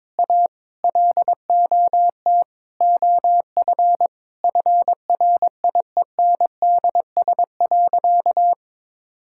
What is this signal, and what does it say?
Morse code, heard as a keyed tone, about 22 wpm, 700 Hz. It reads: A LOT OF FRIENDS.
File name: practice / 22wpm / Sentences with 4-Word Phrases - 22wpm